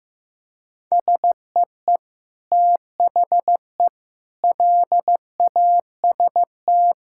Morse code, heard as a keyed tone, about 15 wpm, 700 Hz. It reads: SEE THE LAST
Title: Sentences from Top 1000 Words - Review Part 5 - 15wpm